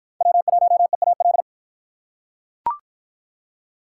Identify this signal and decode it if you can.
Morse code, heard as a keyed tone, about 44 wpm, 700 Hz. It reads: W1EAL